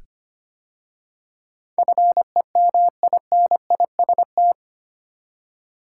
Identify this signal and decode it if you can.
Morse code, heard as a keyed tone, about 25 wpm, 700 Hz. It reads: FEMINIST
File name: practice / 25wpm / 8 Letter Words - First 3 Letters Given - 25wpm